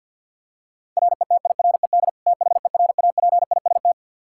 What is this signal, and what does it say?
Morse code, heard as a keyed tone, about 52 wpm, 700 Hz. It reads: RETIRED THERAPIST
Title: Practice QSO - Exchange 3 - More Info - 50wpm